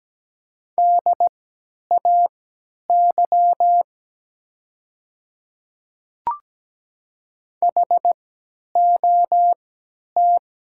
Morse code, heard as a keyed tone, about 17 wpm, spaced as for 10 wpm, 700 Hz. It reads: DAY HOT
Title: 3x Character Spacing - Top 100 Words - 17wpm